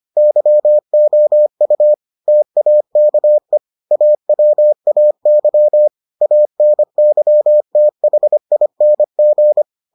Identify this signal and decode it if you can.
Morse code, heard as a keyed tone, about 25 wpm, 600 Hz. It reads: YOU TAKE AWAY ANYTHING